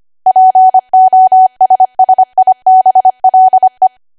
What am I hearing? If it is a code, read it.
Morse code, heard as a keyed tone, about 25 wpm, 750 Hz. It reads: POSSIBLE
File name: Archive / 25 WPM / 25WPM